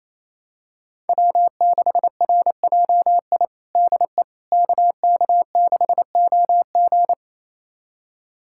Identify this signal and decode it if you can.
Morse code, heard as a keyed tone, about 28 wpm, 700 Hz. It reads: W6RJI DE KK6OG
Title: USA Callsign DE Callsign - 5 Characters - 28wpm